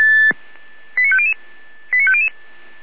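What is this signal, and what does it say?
Five-tone selective call (ZVEI), heard as quick runs of five tones, tones of about 70 ms. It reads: 89500 89500
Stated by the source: ZVEI five-tone selective call received by SDR.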